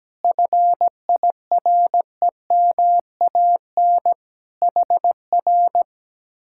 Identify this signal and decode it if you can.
Morse code, heard as a keyed tone, about 17 wpm, 700 Hz. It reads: FIREMAN HR
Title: Speed-Racing Common QSO Phrases Encore - 17wpm